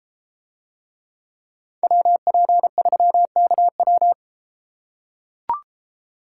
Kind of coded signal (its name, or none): Morse code